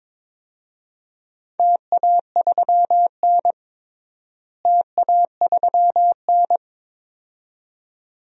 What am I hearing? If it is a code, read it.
Morse code, heard as a keyed tone, about 22 wpm, 700 Hz. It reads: TA3N TA3N
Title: DXCC-100 Call Signs - 4 Characters - Repeated 2x - 22wpm